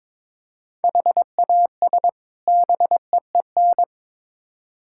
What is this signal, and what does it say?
Morse code, heard as a keyed tone, about 22 wpm, 700 Hz. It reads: HAS BEEN
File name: Build A Phrase - 2-Word Phrases - 22wpm